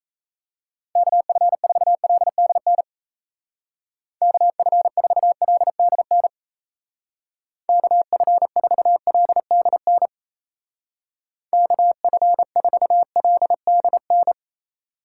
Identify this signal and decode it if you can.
Morse code, heard as a keyed tone, about 33 wpm, 700 Hz. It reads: KF4LDN KF4LDN KF4LDN KF4LDN
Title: Speed Racing - USA 2x3 Call Signs 28wpm